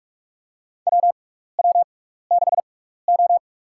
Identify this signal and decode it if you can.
Morse code, heard as a keyed tone, about 42 wpm, spaced as for 15 wpm, 700 Hz. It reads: WW6O
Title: USA Call Signs - 4 Characters - 6x Character Spacing - 45wpm